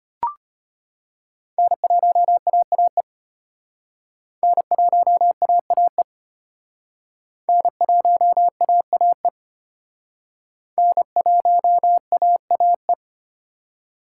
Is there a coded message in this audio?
Yes: Morse code from 1.6 to 12.9 s, heard as a keyed tone, about 32 wpm, 700 Hz. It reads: N1AAE N1AAE N1AAE N1AAE